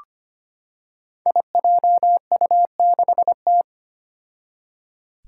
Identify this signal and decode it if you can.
Morse code, heard as a keyed tone, about 25 wpm, 700 Hz. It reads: IJU6T
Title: Five Challenging Letter-Number Combinations 25wpm